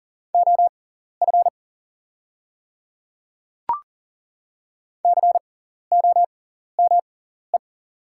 Morse code, heard as a keyed tone, about 37 wpm, spaced as for 14 wpm, 690 Hz. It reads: OF COME
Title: 6x Character Spacing - Top 100 Words - 40wpm